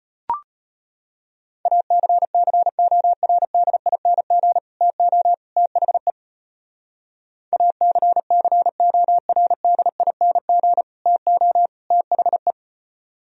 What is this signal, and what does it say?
Morse code, heard as a keyed tone, about 32 wpm, 700 Hz. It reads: ACCORDING TO THE ACCORDING TO THE